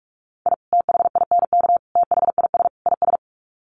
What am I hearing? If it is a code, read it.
Morse code, heard as a keyed tone, about 44 wpm, 700 Hz. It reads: I THINK THIS IS
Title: Sets of 4 Words 45wpm